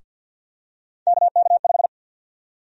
Morse code, heard as a keyed tone, about 47 wpm, 700 Hz. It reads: KK5
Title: USA Call Sign Prefixes - Repeated 2x - 50wpm